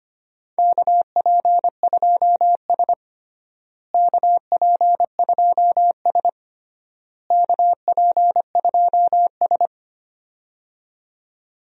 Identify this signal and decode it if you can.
Morse code, heard as a keyed tone, about 25 wpm, 700 Hz. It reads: KP2S KP2S KP2S
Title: DXCC-100 Call Signs - 4 Characters - Repeated 3x - 25wpm